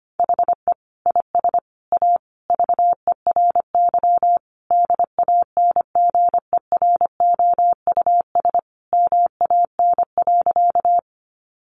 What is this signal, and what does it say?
Morse code, heard as a keyed tone, about 25 wpm, 700 Hz. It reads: HE IS A VERY DANGEROUS MAN.
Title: Sentences from Top 1200 Words 25wpm